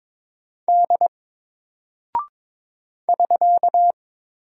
Morse code, heard as a keyed tone, about 22 wpm, 700 Hz. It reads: D <SK>
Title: Single Character (Letter:Number:Punctuation:Prosign) Alt 22wpm